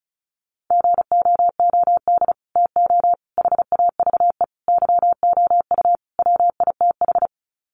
Morse code, heard as a keyed tone, about 34 wpm, 700 Hz. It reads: GOOD TO HAVE YOU WITH
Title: Sets of 5 Words Encore2 35wpm